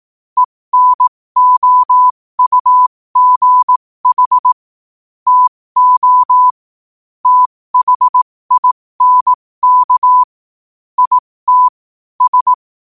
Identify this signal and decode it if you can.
Morse code, heard as a keyed tone, about 18 wpm, spaced as for 15 wpm, 1000 Hz. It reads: ENOUGH TO THINK IT S
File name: qotdfiles / 15wpm